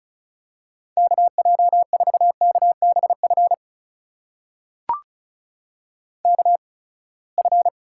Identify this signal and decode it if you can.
Morse code, heard as a keyed tone, about 36 wpm, spaced as for 24 wpm, 700 Hz. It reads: KJ4KBF KF
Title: USA Call Signs - 6 Characters - 8x Character Spacing - 35wpm